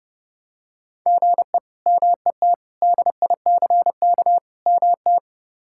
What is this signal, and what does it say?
Morse code, heard as a keyed tone, about 30 wpm, 700 Hz. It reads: GE MET DICK MT
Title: K1USN SST QSO Practice - 30wpm